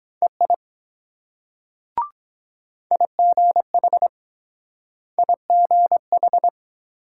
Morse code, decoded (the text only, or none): EI IGH IGH